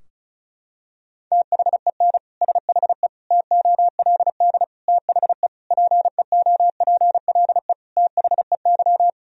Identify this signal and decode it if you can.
Morse code, heard as a keyed tone, about 34 wpm, 700 Hz. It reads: THEN SHE TOLD THE PEOPLE THEY
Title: Sentences from Top 1000 Words - Review Part 2 - 35wpm